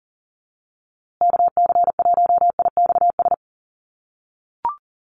Morse code, heard as a keyed tone, about 40 wpm, 700 Hz. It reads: KC1IXS
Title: USA Call Signs Repeated 3x 40wpm